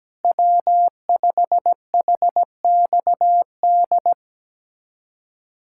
Morse code, heard as a keyed tone, about 17 wpm, 700 Hz. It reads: W5HXD
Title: USA Callsign DE Callsign 17wpm